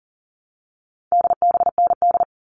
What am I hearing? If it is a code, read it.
Morse code, heard as a keyed tone, about 40 wpm, 700 Hz. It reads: DBND
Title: All Cut Numbers - 4-Digit 40wpm